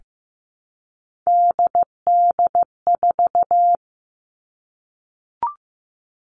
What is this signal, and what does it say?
Morse code, heard as a keyed tone, about 15 wpm, 700 Hz. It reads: DD4
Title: All Cut Numbers - 3-Digit 15wpm